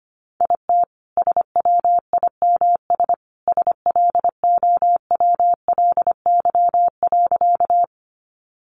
Morse code, heard as a keyed tone, about 25 wpm, 700 Hz. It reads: IT SWIMS SLOWLY.
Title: Additional Easy Phrases 25wpm